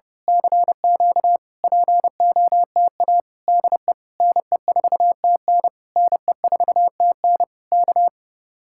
Morse code, heard as a keyed tone, about 28 wpm, 700 Hz. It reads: CQ POTA DE NE4TN NE4TN K
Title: Speed-Racing Common QSO Phrases - 30wpm